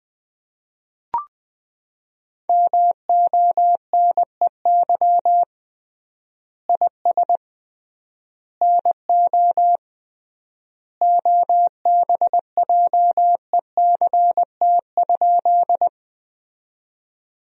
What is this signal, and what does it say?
Morse code, heard as a keyed tone, about 20 wpm, 700 Hz. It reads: MONEY IS NO OBJECT?
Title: 3x Word-Spacing - Sentences from Top 500 Words - 20wpm